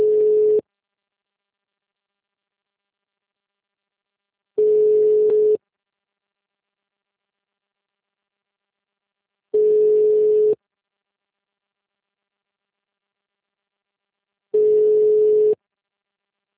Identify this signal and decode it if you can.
Ringback tone (Europe, 425 Hz).